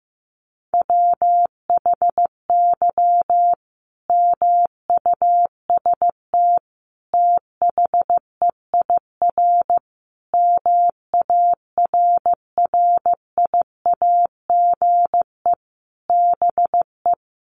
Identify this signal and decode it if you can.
Morse code, heard as a keyed tone, about 15 wpm, 700 Hz. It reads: WHY MUST THEIR MARRIAGE BE
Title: Sentences from Top 1000 Words 15wpm